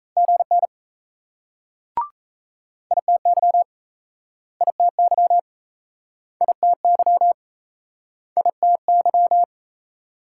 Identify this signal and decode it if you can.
Morse code, heard as a keyed tone, about 37 wpm, 700 Hz. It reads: GN ITY ITY ITY ITY